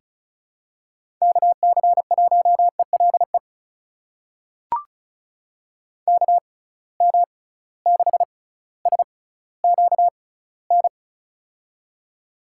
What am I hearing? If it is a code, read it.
Morse code, heard as a keyed tone, about 36 wpm, spaced as for 18 wpm, 700 Hz. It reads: KC1ELE KM6SQN
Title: USA Call Signs - 6 Characters - 6x Character Spacing - 35wpm